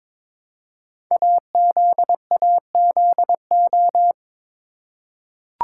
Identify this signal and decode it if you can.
Morse code, heard as a keyed tone, about 22 wpm, 700 Hz. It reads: AZAZO
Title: Code Groups - 5 Letters - 22wpm